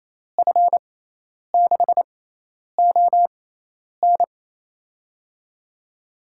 Morse code, heard as a keyed tone, about 28 wpm, spaced as for 10 wpm, 700 Hz. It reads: F6ON
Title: USA Call Signs - 6x Character Spacing - 28wpm